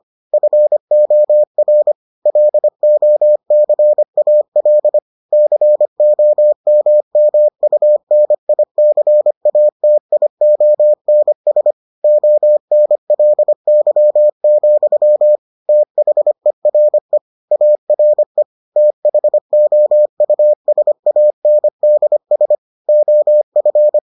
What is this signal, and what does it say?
Morse code, heard as a keyed tone, about 25 wpm, 600 Hz. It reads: FOR LOCAL COMMUNICATIONS ONLY, THERE ARE THOUSANDS OF